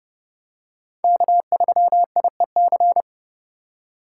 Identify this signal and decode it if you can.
Morse code, heard as a keyed tone, about 31 wpm, 700 Hz. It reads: K3IEC